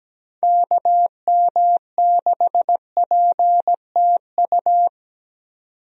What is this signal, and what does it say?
Morse code, heard as a keyed tone, about 17 wpm, 700 Hz. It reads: KM6PTU
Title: USA Call Signs - 6 Characters - 17wpm